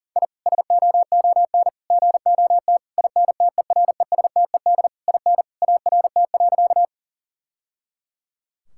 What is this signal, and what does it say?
Morse code, heard as a keyed tone, about 40 wpm, 700 Hz. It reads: I SOON GOT INTERESTED IN ART.
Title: Sentences from Top 1000 Words - Review Part 6 - 40wpm